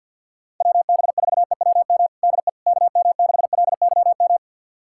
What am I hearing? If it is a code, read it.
Morse code, heard as a keyed tone, about 47 wpm, 700 Hz. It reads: WB4EWM DE KM6LYM
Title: USA Callsign DE Callsign 50wpm